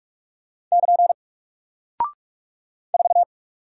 Morse code, heard as a keyed tone, about 44 wpm, 700 Hz. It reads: <KN> 4
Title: Single Character (Letter:Number:Punctuation:Prosign) Alt 45wpm